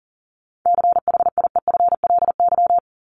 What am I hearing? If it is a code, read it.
Morse code, heard as a keyed tone, about 40 wpm, 700 Hz. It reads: CHIEFLY
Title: LY Suffix - Top 100 Words with Base Words 40wpm